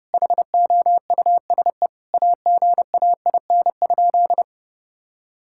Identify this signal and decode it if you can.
Morse code, heard as a keyed tone, about 30 wpm, 700 Hz. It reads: HOUSE AGAIN?
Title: Sentences from Top 1000 Words - Review Part 6 - 30wpm